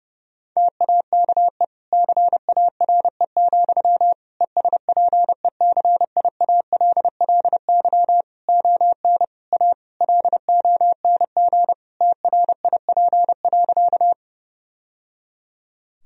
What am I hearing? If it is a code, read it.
Morse code, heard as a keyed tone, about 30 wpm, 700 Hz. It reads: TAKE CARE, ESPECIALLY ON A LONG TRIP.
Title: Sentences from Top 1000 Words - Review Part 4 - 30wpm